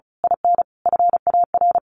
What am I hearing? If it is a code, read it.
Morse code, heard as a keyed tone, about 34 wpm, 700 Hz. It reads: IN FAR
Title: A Sound Of Thunder 35wpm